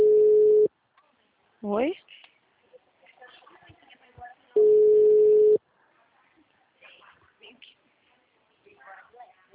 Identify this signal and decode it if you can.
Ringback tone (Europe, 425 Hz), with speech.